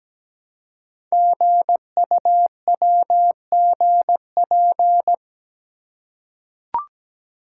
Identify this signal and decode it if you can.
Morse code, heard as a keyed tone, about 17 wpm, 700 Hz. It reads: GUWGP
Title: Code Groups - 5 Letters - 17wpm